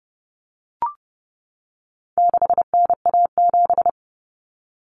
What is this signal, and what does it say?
Morse code, heard as a keyed tone, about 30 wpm, 700 Hz. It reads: 6NA7
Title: Common Cut Numbers - 4-Digit 30wpm